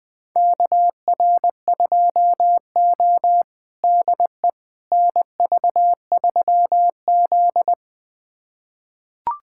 Morse code, heard as a keyed tone, about 21 wpm, 700 Hz. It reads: KR2O DE NV3Z
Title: USA Callsign DE Callsign - 4 Characters - 20wpm